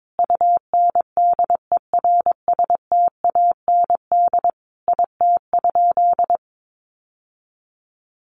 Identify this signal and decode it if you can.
Morse code, heard as a keyed tone, about 22 wpm, 700 Hz. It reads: UNDERSTAND IT?
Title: Sentences from Top 600 Words 22wpm